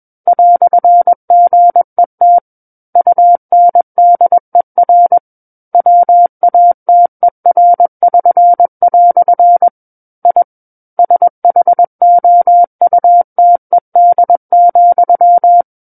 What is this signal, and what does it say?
Morse code, heard as a keyed tone, about 21 wpm, 700 Hz. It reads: "GET UNDER WATER<SN>" I SHOUTED,
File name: Book One / 12-Chapter 12 - 21 WPM